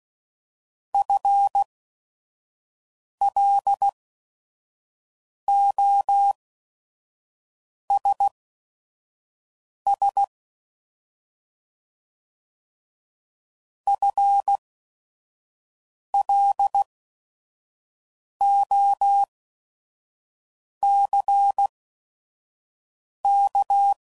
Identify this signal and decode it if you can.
Morse code, heard as a keyed tone, about 16 wpm, spaced as for 5 wpm, 800 Hz. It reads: FLOSS FLOCK
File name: Prefixes / By First Letter / Fs